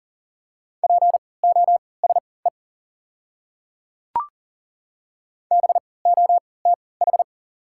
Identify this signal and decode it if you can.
Morse code, heard as a keyed tone, about 37 wpm, spaced as for 22 wpm, 690 Hz. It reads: POSE BOTH
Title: 3x Character Spacing - Top 500 Words - 40wpm